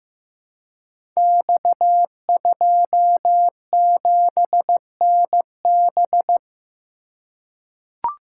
Morse code, heard as a keyed tone, about 15 wpm, 700 Hz. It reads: X27NB
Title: Code Groups - 5 Letter:Numbers - 15wpm